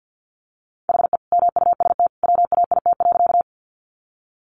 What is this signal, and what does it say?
Morse code, heard as a keyed tone, about 50 wpm, 700 Hz. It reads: HE MUST WAIT.
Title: Sentences from Top 500 Words 50wpm